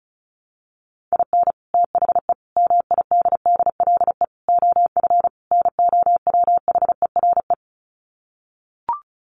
Morse code, heard as a keyed tone, about 34 wpm, 700 Hz. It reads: IN THE MIDDLE OF NOWHERE
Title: Sets of 5 Words Encore 35wpm